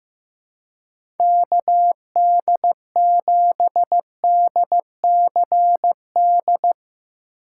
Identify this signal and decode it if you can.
Morse code, heard as a keyed tone, about 16 wpm, 700 Hz. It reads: KD7DCD
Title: USA Call Signs - 6 Characters - 4x Character Spacing - 15wpm